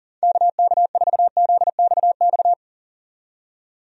Morse code, heard as a keyed tone, about 39 wpm, 700 Hz. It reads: KK4ZXX
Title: USA Call Signs - 6 Characters - 40wpm